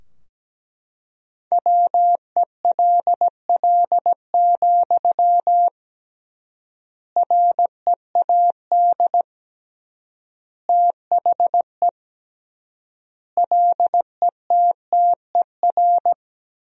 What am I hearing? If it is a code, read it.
Morse code, heard as a keyed tone, about 17 wpm, 700 Hz. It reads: WELL, READ THE LETTER
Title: 3x Word-Spacing - Sentences from Top 300 Words - 17wpm